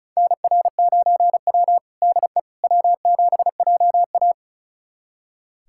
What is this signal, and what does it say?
Morse code, heard as a keyed tone, about 34 wpm, 700 Hz. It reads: NR9W DE W7JA
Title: USA Callsign DE Callsign - 4 Characters - 35wpm